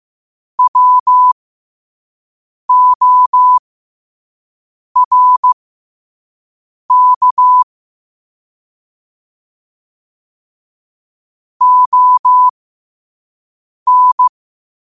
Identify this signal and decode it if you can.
Morse code, heard as a keyed tone, about 15 wpm, spaced as for 5 wpm, 1000 Hz. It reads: WORK ON
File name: qotdfiles / 5wpm